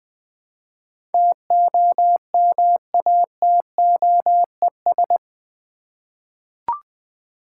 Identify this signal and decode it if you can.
Morse code, heard as a keyed tone, about 20 wpm, 700 Hz. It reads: TOMATOES
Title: Easy Words with Hints - 20wpm